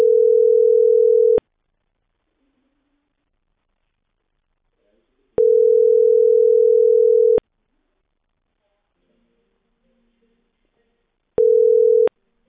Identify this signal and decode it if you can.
Ringback tone (North America), 440 + 480 Hz.
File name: outgoing recordings